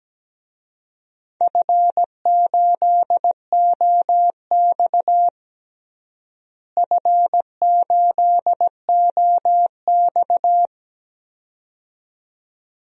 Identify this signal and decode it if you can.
Morse code, heard as a keyed tone, about 17 wpm, 700 Hz. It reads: F8OX F8OX
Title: DXCC-100 Call Signs - 4 Characters - Repeated 2x - 17wpm